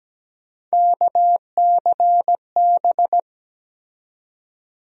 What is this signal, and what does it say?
Morse code, heard as a keyed tone, about 17 wpm, 700 Hz. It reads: KCB
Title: USA Call Sign Postfixes - 3 Letters 17wpm